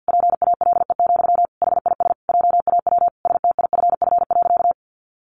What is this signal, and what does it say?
Morse code, heard as a keyed tone, about 50 wpm, 700 Hz. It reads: PALE, HIS JAW STIFF.
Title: A Sound Of Thunder 50wpm